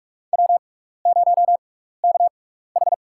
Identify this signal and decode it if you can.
Morse code, heard as a keyed tone, about 46 wpm, spaced as for 15 wpm, 700 Hz. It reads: W0KH
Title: USA Call Signs - 6x Character Spacing - 45wpm